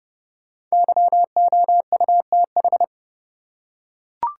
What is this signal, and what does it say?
Morse code, heard as a keyed tone, about 29 wpm, 700 Hz. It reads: YOUTH
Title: Five Letter Words - 30wpm